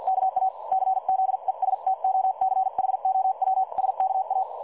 Morse code, heard as a keyed tone, about 50 wpm, spaced as for 43 wpm, 760 Hz. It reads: KA CZESTOCHOWSKA
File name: dlazaawansowanych / nagraniazpasma / websdr recording 3565.0kHz